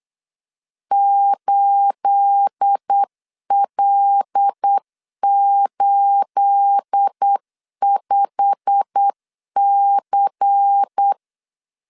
Morse code, heard as a keyed tone, about 8 wpm, 800 Hz. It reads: 8L85C
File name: LEZIONI / 9 28 8wpm 3512